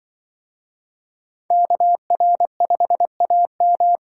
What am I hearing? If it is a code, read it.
Morse code, heard as a keyed tone, about 25 wpm, 700 Hz. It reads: KR5AM
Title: Speed Racing - USA 2x2 Call Signs 20wpm